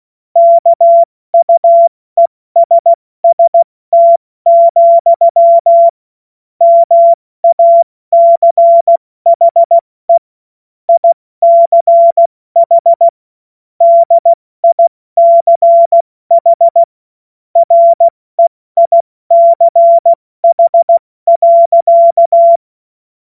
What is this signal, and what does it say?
Morse code, heard as a keyed tone, about 16 wpm, 680 Hz. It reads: KUESST, MACHE ICH DICH REICH.
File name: Morsekurs / Witze / Witz 08. November 2022 70 F80